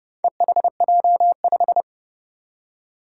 Morse code, heard as a keyed tone, about 30 wpm, 700 Hz. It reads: EHJ5
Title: Four Challenging Letter-Number Combinations 30wpm